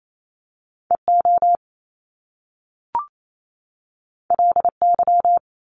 Morse code, heard as a keyed tone, about 28 wpm, 700 Hz. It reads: EO LY
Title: Top Two Letter Combinations in Words 28wpm